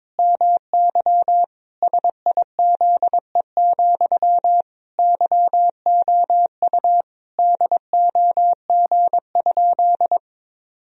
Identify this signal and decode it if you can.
Morse code, heard as a keyed tone, about 22 wpm, 700 Hz. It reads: MY SIZE, YOU DOG?
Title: Sentences from Top 1000 Words - Review Part 2 - 22wpm